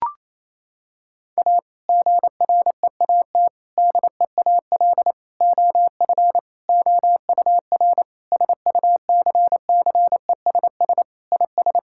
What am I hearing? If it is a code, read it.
Morse code, heard as a keyed tone, about 28 wpm, 690 Hz. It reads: A GREAT DEAL OF OUR SUCCESS IS
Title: Sentences from Top 1000 Words - Review Part 6 - 28wpm